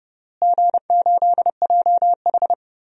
Morse code, heard as a keyed tone, about 32 wpm, 700 Hz. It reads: G8JH
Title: DXCC-100 Call Signs - 4 Characters - Repeated 3x - 30wpm